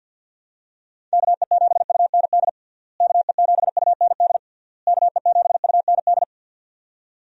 Morse code, heard as a keyed tone, about 49 wpm, 700 Hz. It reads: KE7UND KE7UND KE7UND
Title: USA Call Signs - 6 Characters - Repeated 3x - 50wpm